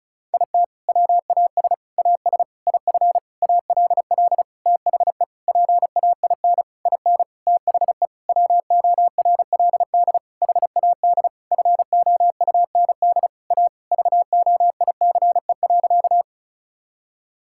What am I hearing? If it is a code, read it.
Morse code, heard as a keyed tone, about 36 wpm, 700 Hz. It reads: IT WAS AS IF ALL THE PAIN IN THE WORLD HAD FOUND A VOICE.
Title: Sentences from Top 1600 Words 35wpm